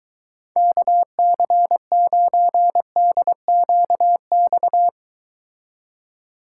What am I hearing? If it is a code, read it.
Morse code, heard as a keyed tone, about 23 wpm, 700 Hz. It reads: KC9DQX